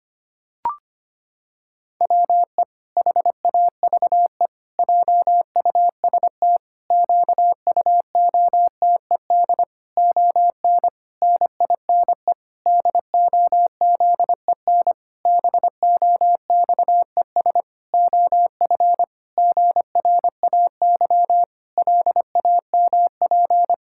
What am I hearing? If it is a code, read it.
Morse code, heard as a keyed tone, about 25 wpm, 700 Hz. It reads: WE HAVE JUST QUOTED ON NINE DOZEN BOXES OF GRAY LAMP
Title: Pangram Challenge v2 - 25wpm